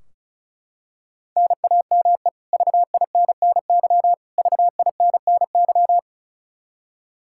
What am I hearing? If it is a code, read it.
Morse code, heard as a keyed tone, about 36 wpm, 700 Hz. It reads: NAME VINNY VINNY
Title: Practice QSO - Exchange 1 - Beginning - 35wpm